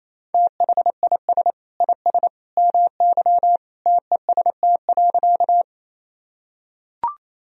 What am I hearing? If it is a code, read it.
Morse code, heard as a keyed tone, about 27 wpm, 690 Hz. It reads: THIS IS MY TEST.
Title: Sentences from Top 1000 Words - Review Part 3 - 28wpm